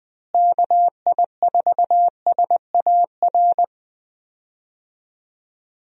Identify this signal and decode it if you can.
Morse code, heard as a keyed tone, about 20 wpm, 700 Hz. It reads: KI4SAR
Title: USA Call Signs - 6 Characters - Repeated 3x - 20wpm